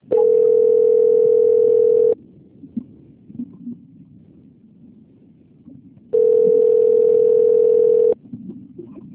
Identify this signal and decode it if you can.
Ringback tone (North America), 440 + 480 Hz.
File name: out- - - - - .6061